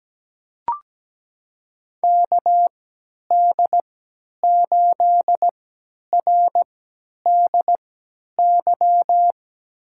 Morse code, heard as a keyed tone, about 17 wpm, spaced as for 10 wpm, 700 Hz. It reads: KD8RDY
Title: USA Call Signs - 6 Characters - 3x Character Spacing - 17wpm